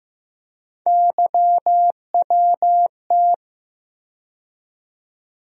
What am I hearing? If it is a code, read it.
Morse code, heard as a keyed tone, about 15 wpm, 700 Hz. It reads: YWT
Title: USA Call Sign Postfixes - 3 Letters 15wpm